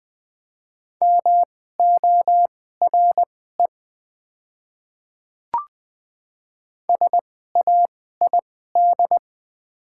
Morse code, heard as a keyed tone, about 20 wpm, spaced as for 14 wpm, 700 Hz. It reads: MORE SAID
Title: 2x Character Spacing - Top 300 Words - 20wpm